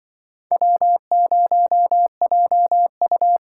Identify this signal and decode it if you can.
Morse code, heard as a keyed tone, about 25 wpm, 700 Hz. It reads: W0JU